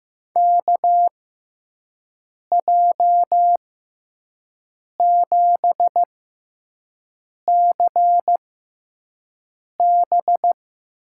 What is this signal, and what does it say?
Morse code, heard as a keyed tone, about 15 wpm, spaced as for 5 wpm, 700 Hz. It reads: KJ7CB
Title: USA Call Signs - 6 Characters - 6x Character Spacing - 15wpm